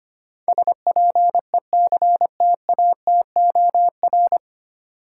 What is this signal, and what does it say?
Morse code, heard as a keyed tone, about 25 wpm, 700 Hz. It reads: SPECTATOR
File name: practice / 25wpm / 9 Letter Words - First 3 Letters Given - 25wpm